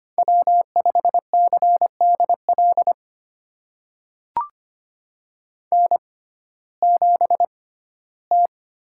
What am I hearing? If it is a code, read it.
Morse code, heard as a keyed tone, about 25 wpm, spaced as for 15 wpm, 700 Hz. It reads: W5CDL N7T
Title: USA Call Signs - 5 Characters - 6x Character Spacing - 25wpm